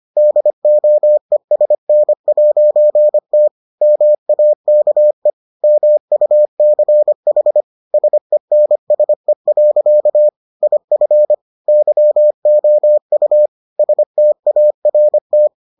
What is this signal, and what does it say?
Morse code, heard as a keyed tone, about 25 wpm, 600 Hz. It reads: DOESN'T MAKE MUCH SENSE. IF YOU START